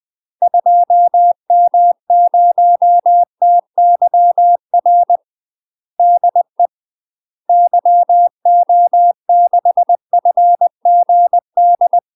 Morse code, heard as a keyed tone, about 20 wpm, 700 Hz. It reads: 2M0TYR DE YO6FGD
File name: UK MASTER QSOs TRAINING / 20WPM / UK MASTER QSOs 0070